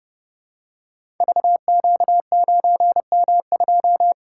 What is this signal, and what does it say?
Morse code, heard as a keyed tone, about 30 wpm, 700 Hz. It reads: VQ9M2
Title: Code Groups - 5 Letter:Numbers - 30wpm